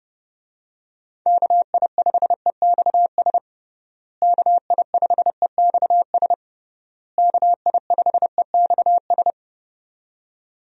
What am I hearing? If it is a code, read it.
Morse code, heard as a keyed tone, about 30 wpm, 700 Hz. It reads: KI5EXS KI5EXS KI5EXS
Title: USA Call Signs - 6 Characters - Repeated 3x - 30wpm